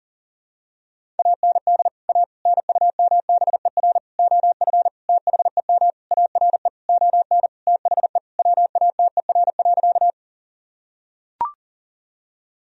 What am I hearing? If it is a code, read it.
Morse code, heard as a keyed tone, about 37 wpm, 700 Hz. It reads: AND A NUMBER OF THEM ARE ON THE WATER.
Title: Sentences from Top 1000 Words - Review Part 1 - 40wpm